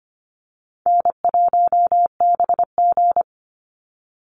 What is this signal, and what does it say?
Morse code, heard as a keyed tone, about 25 wpm, 700 Hz. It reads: N1BG
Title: Easy Call Signs Repeated 3x - 25wpm